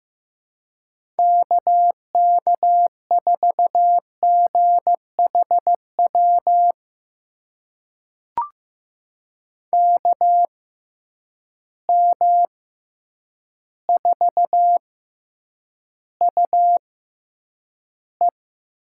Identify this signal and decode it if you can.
Morse code, heard as a keyed tone, about 15 wpm, spaced as for 8 wpm, 700 Hz. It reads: KK4GHW KM4UE
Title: USA Call Signs - 6 Characters - 6x Character Spacing - 15wpm